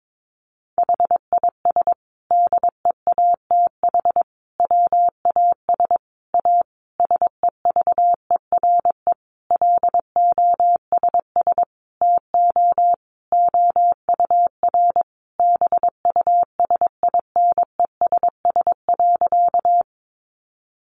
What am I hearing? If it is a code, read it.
Morse code, heard as a keyed tone, about 22 wpm, 700 Hz. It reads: HIS DEATH WAS A SEVERE LOSS TO OUR BUSINESS.
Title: Sentences from Top 1400 Words 22wpm